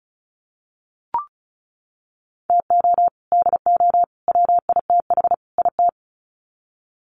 Morse code, heard as a keyed tone, about 34 wpm, 700 Hz. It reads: TO DO WITH IT
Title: Sets of 4 Words 35wpm